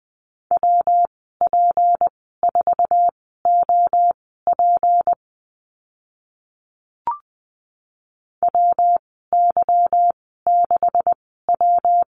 Morse code, heard as a keyed tone, about 20 wpm, spaced as for 15 wpm, 700 Hz. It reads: WP4OP WY6W
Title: 154 Lesson 25g - Letter p - Call Signs 20wpm